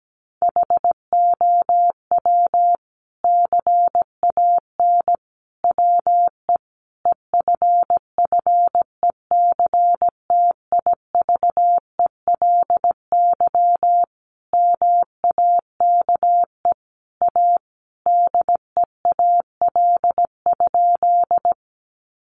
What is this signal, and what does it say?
Morse code, heard as a keyed tone, about 17 wpm, 700 Hz. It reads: HOW CAN WE EFFECTIVELY MAKE A DEAL?
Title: Sentences from Top 1400 Words 17wpm